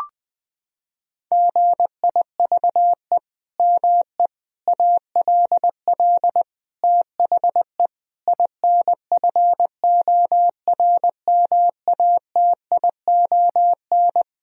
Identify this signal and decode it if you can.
Morse code, heard as a keyed tone, about 20 wpm, 700 Hz. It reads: GIVE ME ALL THE INFORMATION
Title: Sentences from Top 1000 Words - Review Part 3 - 20wpm